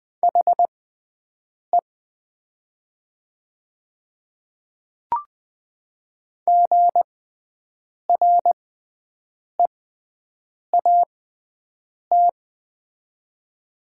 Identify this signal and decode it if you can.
Morse code, heard as a keyed tone, about 20 wpm, spaced as for 7 wpm, 700 Hz. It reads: HE GREAT
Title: 6x Character Spacing - Top 300 Words - 20wpm